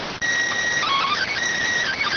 SSTV (slow-scan television), PD 290.